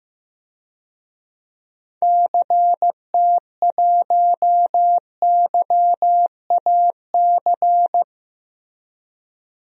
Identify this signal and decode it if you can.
Morse code, heard as a keyed tone, about 15 wpm, 700 Hz. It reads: CT1YAC
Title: DXCC-100 Call Signs - 6 Characters - Repeated 2x - 15wpm